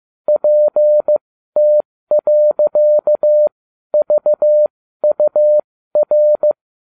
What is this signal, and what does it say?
Morse code, heard as a keyed tone, about 15 wpm, spaced as for 12 wpm, 600 Hz. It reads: PT.VUR